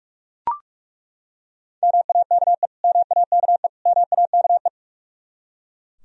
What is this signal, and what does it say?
Morse code, heard as a keyed tone, about 43 wpm, 700 Hz. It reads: MAKE MAKE MAKE
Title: Top 100 Words Repeated 45wpm